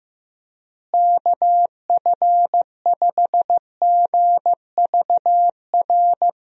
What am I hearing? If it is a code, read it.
Morse code, heard as a keyed tone, about 16 wpm, 700 Hz. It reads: KF5GVR
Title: USA Call Signs - 6 Characters - Repeated 3x - 15wpm